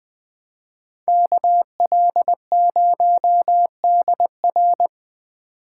Morse code, heard as a keyed tone, about 20 wpm, 700 Hz. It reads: KL0DR
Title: Speed Racing - USA 2x2 Call Signs 17wpm